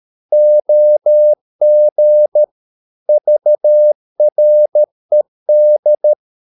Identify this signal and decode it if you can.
Morse code, heard as a keyed tone, about 13 wpm, 600 Hz. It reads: OG VRED
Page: Ferie 13 wpm